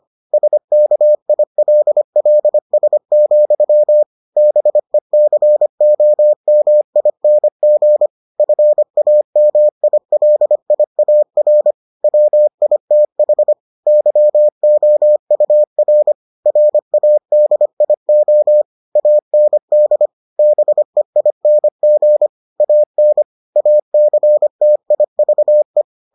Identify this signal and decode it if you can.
Morse code, heard as a keyed tone, about 25 wpm, 600 Hz. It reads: SKILLS, BECOMING FAMILIAR WITH YOUR RADIO AND BEING AN ACTIVE